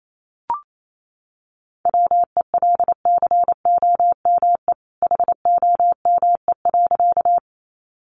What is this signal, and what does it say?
Morse code, heard as a keyed tone, about 28 wpm, 700 Hz. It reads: WELCOME HOME.
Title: A Sound Of Thunder 28wpm